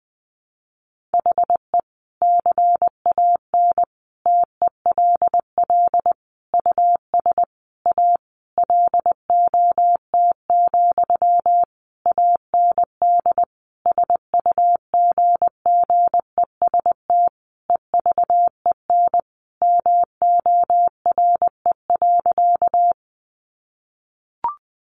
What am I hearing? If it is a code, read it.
Morse code, heard as a keyed tone, about 20 wpm, 700 Hz. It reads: HE CAN TELL US A LOT, AND SUGGEST EVEN MORE.
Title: Sentences from Top 1300 Words 20wpm